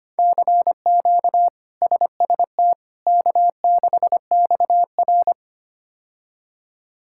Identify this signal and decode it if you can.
Morse code, heard as a keyed tone, about 25 wpm, 700 Hz. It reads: CQ SST K6XR
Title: K1USN SST QSO Practice - 25wpm